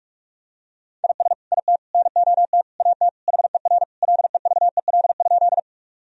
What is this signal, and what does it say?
Morse code, heard as a keyed tone, about 42 wpm, 700 Hz. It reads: IS IT NOT AT HER LEVEL?
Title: Sentences from Top 1000 Words - Review Part 4 - 45wpm